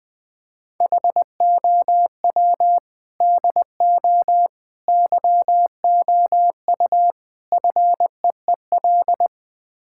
Morse code, heard as a keyed tone, about 21 wpm, 700 Hz. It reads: HOW DO YOU FEEL
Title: Build A Phrase - 5-Word Phrases - 20wpm